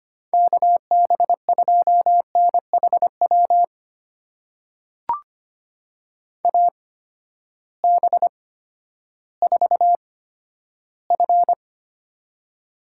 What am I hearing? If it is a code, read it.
Morse code, heard as a keyed tone, about 25 wpm, spaced as for 12 wpm, 700 Hz. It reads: KB2NHW AB4F
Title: USA Call Signs - 8x Character Spacing - 25wpm